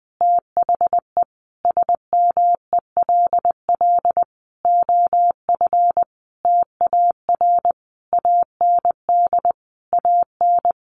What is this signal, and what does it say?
Morse code, heard as a keyed tone, about 20 wpm, 700 Hz. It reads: THE SMELL OF TAR AND AN
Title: A Sound Of Thunder 20wpm